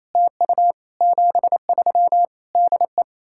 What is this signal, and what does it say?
Morse code, heard as a keyed tone, about 38 wpm, 700 Hz. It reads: TU 73 DE